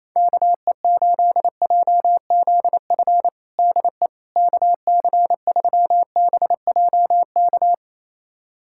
Morse code, heard as a keyed tone, about 27 wpm, 700 Hz. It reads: KE8JZF DE KC3BJK
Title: USA Callsign DE Callsign - 6 Characters - 28wpm